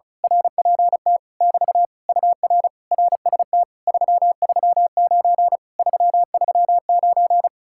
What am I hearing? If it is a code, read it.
Morse code, heard as a keyed tone, about 34 wpm, 700 Hz. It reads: RPT <BT> UR RST 339 339
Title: Speed-Racing Common QSO Phrases - 35wpm